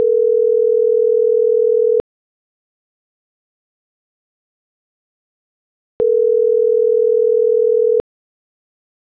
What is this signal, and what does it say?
Ringback tone (North America), 440 + 480 Hz.